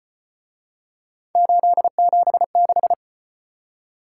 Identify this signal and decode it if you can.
Morse code, heard as a keyed tone, about 34 wpm, 700 Hz. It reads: 876